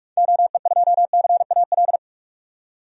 Morse code, heard as a keyed tone, about 44 wpm, 700 Hz. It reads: OE2CAL
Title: DXCC-100 Call Signs - 6 Characters - 45wpm